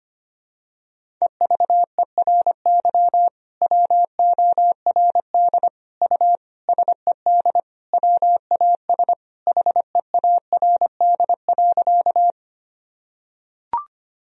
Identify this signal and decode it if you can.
Morse code, heard as a keyed tone, about 25 wpm, 700 Hz. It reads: EVERY WORD U SED WAS HEARD.
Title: Sentences from Top 400 Words and Abbreviations 25wpm